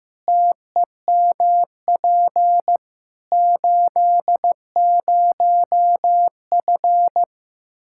Morse code, heard as a keyed tone, about 15 wpm, 700 Hz. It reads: TEMP 80F
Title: Practice QSO - Exchange 2 - Info - 15wpm